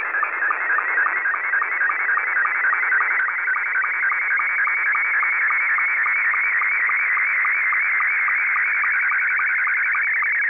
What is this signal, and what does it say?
SSTV (slow-scan television), Scottie 2.
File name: SSTV images / websdr recording start 14230.0kHz